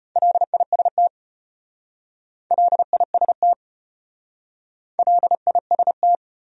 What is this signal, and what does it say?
Morse code, heard as a keyed tone, about 33 wpm, 700 Hz. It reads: LIST LIST LIST